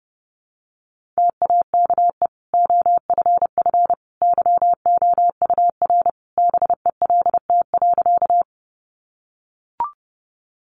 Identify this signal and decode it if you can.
Morse code, heard as a keyed tone, about 30 wpm, 700 Hz. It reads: TAKE OFF YOUR BELT.
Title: Sentences from Top 1400 Words 30wpm